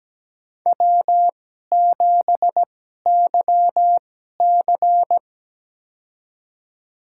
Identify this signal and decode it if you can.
Morse code, heard as a keyed tone, about 17 wpm, spaced as for 12 wpm, 700 Hz. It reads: W7YC
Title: USA Call Signs - 2x Character Spacing - 17wpm